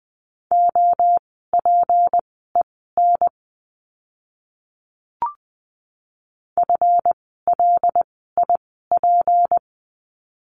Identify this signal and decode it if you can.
Morse code, heard as a keyed tone, about 20 wpm, spaced as for 15 wpm, 700 Hz. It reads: OPEN FLIP
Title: 153 Lesson 25f - Letter p - Words 20wpm